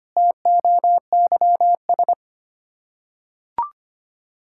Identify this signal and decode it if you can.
Morse code, heard as a keyed tone, about 25 wpm, 700 Hz. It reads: TOYS
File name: practice / 25wpm / Difficult Words with Hints - 25wpm